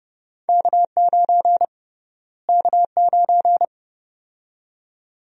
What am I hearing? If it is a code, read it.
Morse code, heard as a keyed tone, about 30 wpm, 700 Hz. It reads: K9 K9
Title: USA Call Sign Prefixes - Repeated 2x - 30wpm